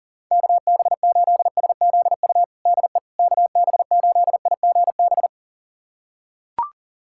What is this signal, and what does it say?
Morse code, heard as a keyed tone, about 37 wpm, 700 Hz. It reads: KB8SZU DE KB8IGB
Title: USA Callsign DE Callsign - 6 Characters - 40wpm